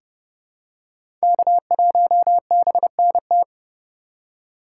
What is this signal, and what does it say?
Morse code, heard as a keyed tone, about 32 wpm, 700 Hz. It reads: K1BNT
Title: USA Call Signs - 5 Characters - 3x Character Spacing - 30wpm